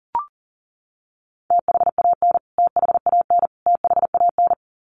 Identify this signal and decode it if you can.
Morse code, heard as a keyed tone, about 38 wpm, 700 Hz. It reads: THAN THAN THAN
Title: Top 200 Words Repeated 40wpm